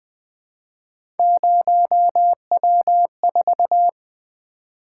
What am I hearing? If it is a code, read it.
Morse code, heard as a keyed tone, about 20 wpm, 700 Hz. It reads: 0W4
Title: Three Letter-Number Combinations - 20wpm